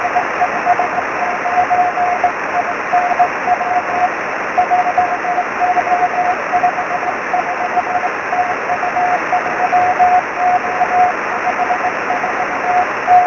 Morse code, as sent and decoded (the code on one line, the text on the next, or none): none